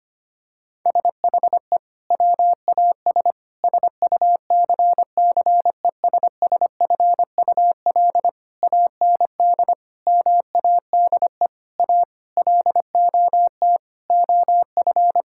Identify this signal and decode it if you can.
Morse code, heard as a keyed tone, about 25 wpm, 700 Hz. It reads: SHE WAS SUCCESSFUL AND MADE A LOT OF
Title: Sentences from Top 1000 Words - Review Part 6 - 25wpm